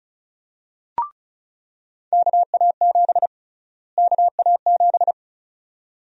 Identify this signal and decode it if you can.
Morse code, heard as a keyed tone, about 34 wpm, 700 Hz. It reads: KA7 KA7
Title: USA Call Sign Prefixes - Repeated 2x - 35wpm